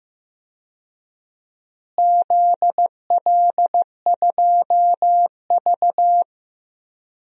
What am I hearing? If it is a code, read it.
Morse code, heard as a keyed tone, about 15 wpm, 700 Hz. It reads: ZL2V
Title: DXCC-100 Call Signs - 4 Characters - Repeated 2x - 15wpm